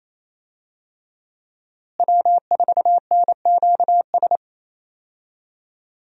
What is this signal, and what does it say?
Morse code, heard as a keyed tone, about 28 wpm, 700 Hz. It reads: W4NQS
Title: USA Call Signs - 5 Characters - Repeated 2x - 28wpm